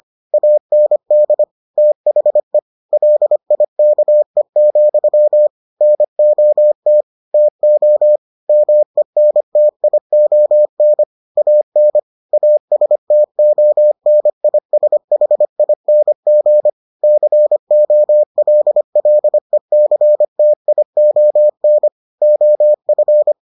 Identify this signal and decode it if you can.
Morse code, heard as a keyed tone, about 25 wpm, 600 Hz. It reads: AND THE LIKE, NOT TO MENTION AN ASTONISHING COLLECTION OF